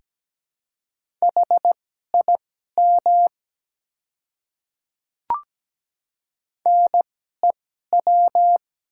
Morse code, heard as a keyed tone, about 17 wpm, spaced as for 12 wpm, 700 Hz. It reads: HIM NEW
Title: 2x Character Spacing - Top 200 Words - 17wpm